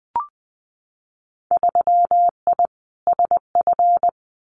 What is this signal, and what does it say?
Morse code, heard as a keyed tone, about 20 wpm, 700 Hz. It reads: 3I SF
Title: Winter Field Day Exchanges 20wpm